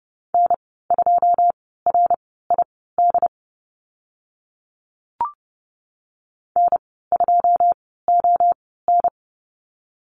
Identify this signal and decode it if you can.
Morse code, heard as a keyed tone, about 30 wpm, spaced as for 17 wpm, 700 Hz. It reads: N2RID N2ON
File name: Lesson 15g - Number 5 - Call Signs 30wpm